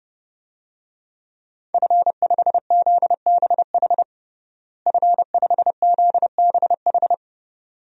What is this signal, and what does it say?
Morse code, heard as a keyed tone, about 32 wpm, 700 Hz. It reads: F5ZBH F5ZBH
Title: DXCC-100 Call Signs - 5 Characters - Repeated 2x - 30wpm